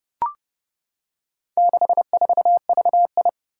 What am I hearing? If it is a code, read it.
Morse code, heard as a keyed tone, about 30 wpm, 700 Hz. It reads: 64VI
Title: Four Challenging Letter-Number Combinations 30wpm